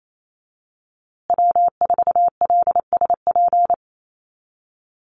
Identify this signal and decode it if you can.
Morse code, heard as a keyed tone, about 28 wpm, 700 Hz. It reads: W4LSP
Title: USA Call Signs Repeated 2x 28wpm